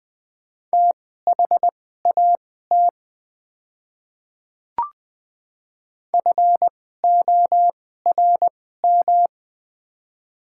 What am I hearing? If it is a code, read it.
Morse code, heard as a keyed tone, about 20 wpm, spaced as for 15 wpm, 700 Hz. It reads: THAT FORM
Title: 2x Character Spacing - Top 200 Words - 20wpm